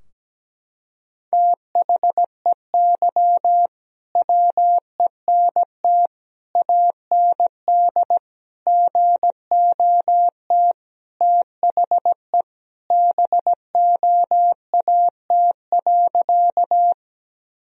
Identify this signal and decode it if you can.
Morse code, heard as a keyed tone, about 17 wpm, 700 Hz. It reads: THEY WENT AND GOT THE BOAT.
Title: Sentences from Top 1000 Words - Review Part 3 - 17wpm